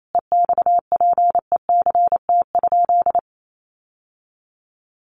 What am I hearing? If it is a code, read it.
Morse code, heard as a keyed tone, about 28 wpm, 700 Hz. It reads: EXPECT?
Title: Sentences from Top 1100 Words 28wpm